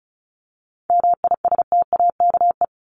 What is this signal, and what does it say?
Morse code, heard as a keyed tone, about 34 wpm, 700 Hz. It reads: MISTAKE
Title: Top 1501-1750 Words 35wpm